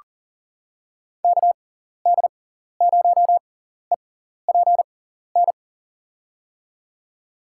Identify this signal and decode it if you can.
Morse code, heard as a keyed tone, about 37 wpm, spaced as for 14 wpm, 700 Hz. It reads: KD0EPN
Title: USA Call Signs - 6 Characters - 6x Character Spacing - 40wpm